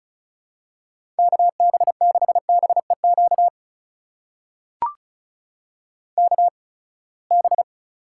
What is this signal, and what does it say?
Morse code, heard as a keyed tone, about 36 wpm, spaced as for 24 wpm, 700 Hz. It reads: KB6BEQ KB
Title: USA Call Signs - 6 Characters - 8x Character Spacing - 35wpm